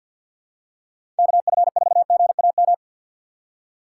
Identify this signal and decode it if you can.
Morse code, heard as a keyed tone, about 49 wpm, 700 Hz. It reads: KF4GAM
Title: USA Call Signs - 6 Characters - 6x Character Spacing - 50wpm